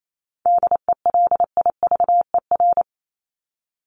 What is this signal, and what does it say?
Morse code, heard as a keyed tone, about 28 wpm, 700 Hz. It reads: DELIVER
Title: Top 1751-2000 Words 28wpm